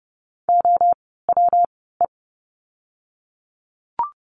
Morse code, heard as a keyed tone, about 30 wpm, 700 Hz. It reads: O W E
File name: course / 30:10wpm / 115 Lesson 19f - Letter w - Words 30wpm